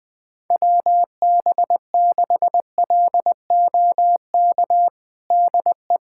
Morse code, heard as a keyed tone, about 20 wpm, 700 Hz. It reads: WB6LOK DE
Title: USA Callsign DE Callsign - 6 Characters - 20wpm